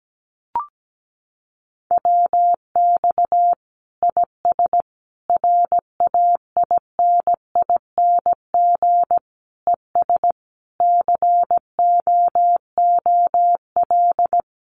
Morse code, heard as a keyed tone, about 17 wpm, 700 Hz. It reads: WX IS RAINING ES COOL
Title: Common QSO Phrases 17wpm